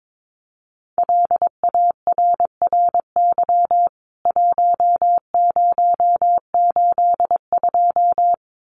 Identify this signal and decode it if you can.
Morse code, heard as a keyed tone, about 22 wpm, 700 Hz. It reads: LARRY 1082
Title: CWT Exchange 22wpm